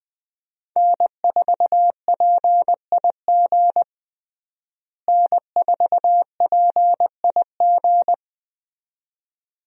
Morse code, heard as a keyed tone, about 21 wpm, 700 Hz. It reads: N4PIG N4PIG
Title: USA Call Signs - 5 Characters - Repeated 2x - 20wpm